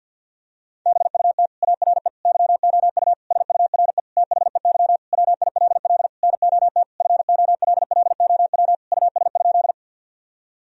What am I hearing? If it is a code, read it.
Morse code, heard as a keyed tone, about 50 wpm, 700 Hz. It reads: BUT ARE YOU SURE THEY WILL NOT FOLLOW US?
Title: Sentences from Top 1000 Words - Review Part 1 - 50wpm